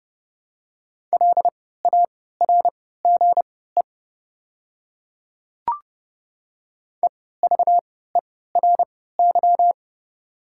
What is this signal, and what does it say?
Morse code, heard as a keyed tone, about 29 wpm, spaced as for 17 wpm, 690 Hz. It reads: LARGE EVERY
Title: 3x Character Spacing - Top 500 Words - 30wpm